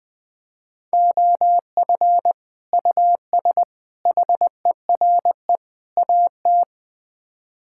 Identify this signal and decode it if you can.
Morse code, heard as a keyed tone, about 20 wpm, 700 Hz. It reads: OF US HERE AT
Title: Build A Phrase - 5-Word Phrases - 20wpm